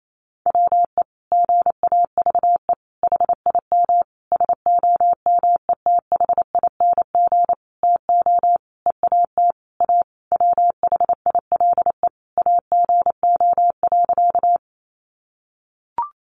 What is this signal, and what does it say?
Morse code, heard as a keyed tone, about 28 wpm, 700 Hz. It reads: WE GAVE HIM SOMETHING TO EAT A WHILE AGO.
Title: Sentences from Top 600 Words 28wpm